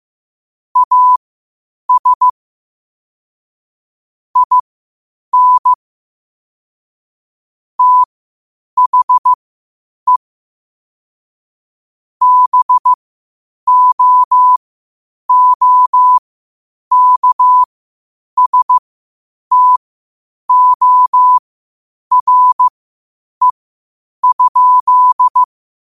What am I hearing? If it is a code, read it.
Morse code, heard as a keyed tone, about 15 wpm, spaced as for 8 wpm, 1000 Hz. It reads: AS IN THE BOOKSTORE?
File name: qotdfiles / 8wpm